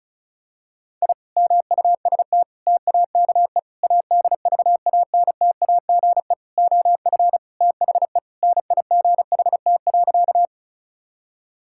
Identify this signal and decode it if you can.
Morse code, heard as a keyed tone, about 36 wpm, 700 Hz. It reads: I MUST TAKE ADVANTAGE OF THE NIGHT.
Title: Sentences from Top 1000 Words - Review Part 6 - 35wpm